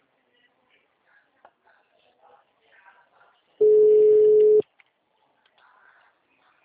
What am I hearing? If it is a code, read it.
Ringback tone (Europe, 425 Hz).